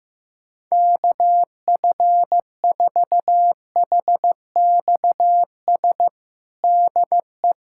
Morse code, heard as a keyed tone, about 15 wpm, 700 Hz. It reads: KF4HXS DE
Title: USA Callsign DE Callsign - 6 Characters - 15wpm